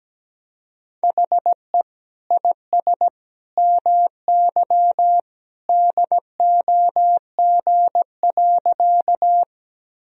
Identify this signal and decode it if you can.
Morse code, heard as a keyed tone, about 17 wpm, 700 Hz. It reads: HE IS MY DOG.
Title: Sentences from Top 1000 Words - Review Part 2 - 17wpm